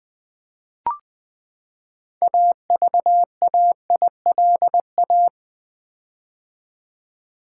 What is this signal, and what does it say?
Morse code, heard as a keyed tone, about 20 wpm, 700 Hz. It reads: AVAILA
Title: Build A Word - 6 Letters At A Time - 12-Letter Words - 20wpm